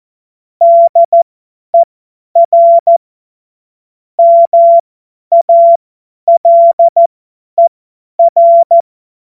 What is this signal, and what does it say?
Morse code, heard as a keyed tone, about 14 wpm, spaced as for 10 wpm, 680 Hz. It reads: DER MALER
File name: Morsekurs / Witze / Witz 19 Dezember 2023 50 F70